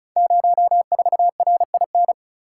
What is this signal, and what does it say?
Morse code, heard as a keyed tone, about 36 wpm, 700 Hz. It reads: 04RIN
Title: Code Groups - 5 Letter:Numbers - 35wpm